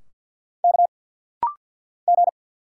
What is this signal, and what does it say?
Morse code, heard as a keyed tone, about 50 wpm, 700 Hz. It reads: K G
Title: Single Letters - Rapid-Fire 50wpm